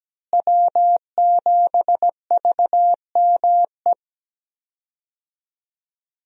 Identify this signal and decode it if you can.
Morse code, heard as a keyed tone, about 17 wpm, 700 Hz. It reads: W7VME
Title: USA Call Signs - 5 Characters - Repeated 3x - 17wpm